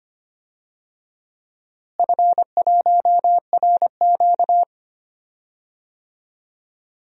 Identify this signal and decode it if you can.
Morse code, heard as a keyed tone, about 25 wpm, 700 Hz. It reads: F1RQ
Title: DXCC-100 Call Signs - 4 Characters - Repeated 2x - 25wpm